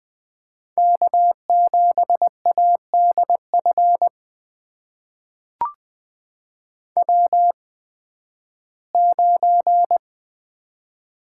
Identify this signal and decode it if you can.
Morse code, heard as a keyed tone, about 20 wpm, spaced as for 13 wpm, 700 Hz. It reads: K7ADF W9
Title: USA Call Signs - 5 Characters - 8x Character Spacing - 20wpm